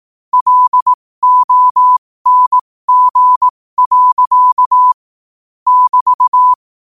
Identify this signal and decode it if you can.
Morse code, heard as a keyed tone, about 18 wpm, spaced as for 15 wpm, 1000 Hz. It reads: LONG. <BT>
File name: qotdfiles / 15wpm